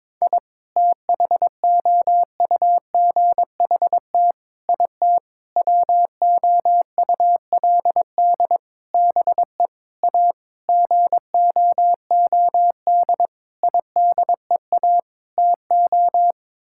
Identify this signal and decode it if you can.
Morse code, heard as a keyed tone, about 22 wpm, 700 Hz. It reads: I THOUGHT IT WOULD BE A GOOD IDEA TO
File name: Sets of 9 Words - 22wpm